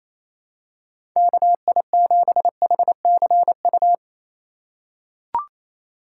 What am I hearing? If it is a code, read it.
Morse code, heard as a keyed tone, about 28 wpm, 700 Hz. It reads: KI7HCU
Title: USA Call Signs - 6 Characters - 3x Character Spacing - 28wpm